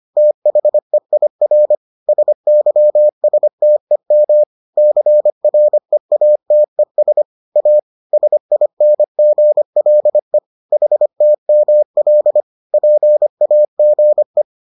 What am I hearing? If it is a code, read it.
Morse code, heard as a keyed tone, about 25 wpm, 600 Hz. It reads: THEIR SYSTEM CREATES A SINGLE HTML PAGE